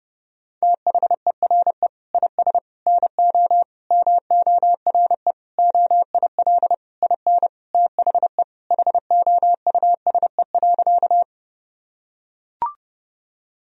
Morse code, heard as a keyed tone, about 30 wpm, 700 Hz. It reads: THERE IS NO MORE OIL IN THE HOUSE.
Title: Sentences from Top 1000 Words - Review Part 4 - 30wpm